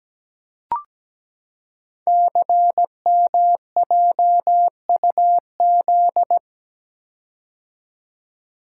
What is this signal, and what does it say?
Morse code, heard as a keyed tone, about 17 wpm, 700 Hz. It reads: CMJUZ
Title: Code Groups - 5 Letters - 17wpm